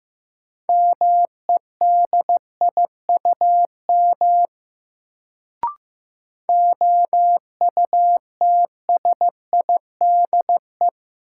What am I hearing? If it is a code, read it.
Morse code, heard as a keyed tone, about 15 wpm, 700 Hz. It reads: MEDIUM OUTSIDE
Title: Sentences from Top 900 Words - New Word Drill - 15wpm